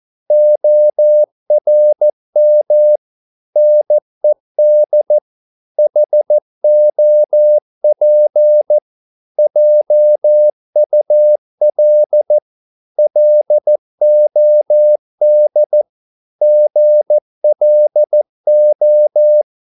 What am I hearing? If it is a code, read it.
Morse code, heard as a keyed tone, about 14 wpm, 600 Hz. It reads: ORM NED HOP JUL LOD GLO
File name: Korte ord DK 14wpm